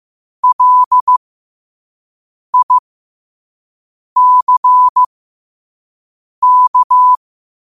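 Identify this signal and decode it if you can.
Morse code, heard as a keyed tone, about 15 wpm, spaced as for 5 wpm, 1000 Hz. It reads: LICK
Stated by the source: Morse code sent at 5 wpm.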